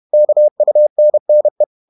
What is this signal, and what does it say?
Morse code, heard as a keyed tone, about 31 wpm, 600 Hz. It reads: KUNNE
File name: Ferie QRQ 31wpm